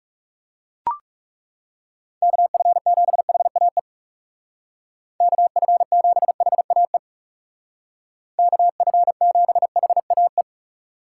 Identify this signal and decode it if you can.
Morse code, heard as a keyed tone, about 39 wpm, 700 Hz. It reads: KF7HAE KF7HAE KF7HAE